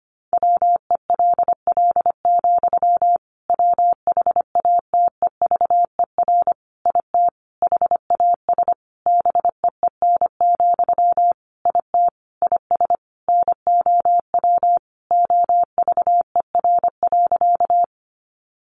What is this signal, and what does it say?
Morse code, heard as a keyed tone, about 25 wpm, 700 Hz. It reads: WELL, WHATEVER IT HAS BEEN, IT IS NOW OVER.
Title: Sentences from Top 1100 Words 25wpm